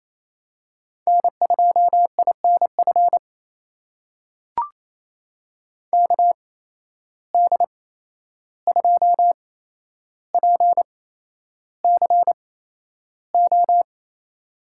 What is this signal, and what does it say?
Morse code, heard as a keyed tone, about 28 wpm, spaced as for 11 wpm, 700 Hz. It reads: N2INF KD2PCO
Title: USA Call Signs - 8x Character Spacing - 28wpm